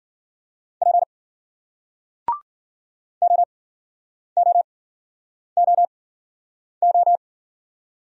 Morse code, heard as a keyed tone, about 55 wpm, spaced as for 9 wpm, 700 Hz. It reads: P OOOO